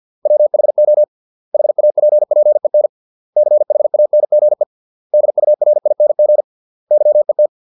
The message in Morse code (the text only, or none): WHO HAPPEN CHANGE DURING YET